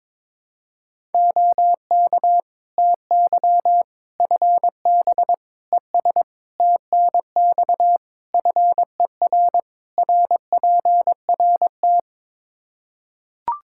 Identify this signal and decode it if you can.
Morse code, heard as a keyed tone, about 22 wpm, 700 Hz. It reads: OK TY FB ES TNX FER RPRT
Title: K1USN SST QSO Practice 22wpm